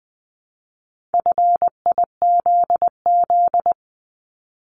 Morse code, heard as a keyed tone, about 20 wpm, 700 Hz. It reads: FIZZ
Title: Multi-letter Phonemes 20wpm